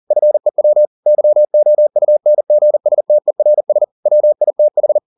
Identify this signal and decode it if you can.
Morse code, heard as a keyed tone, about 40 wpm, 600 Hz. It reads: FEW YOUNGSTERS WITH